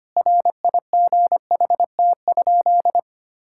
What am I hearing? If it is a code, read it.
Morse code, heard as a keyed tone, about 25 wpm, 700 Hz. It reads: RIGHT?
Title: Sentences from Top 1000 Words - Review Part 5 - 25wpm